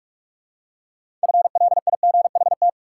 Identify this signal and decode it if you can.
Morse code, heard as a keyed tone, about 41 wpm, 700 Hz. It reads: FLIGHT
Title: Six Letter Words - 45wpm